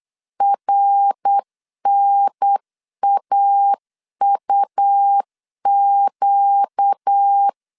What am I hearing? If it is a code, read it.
Morse code, heard as a keyed tone, about 8 wpm, 800 Hz. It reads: RNAUQ
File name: LEZIONI / 7 GF9Z / 7 G 8wpm 3118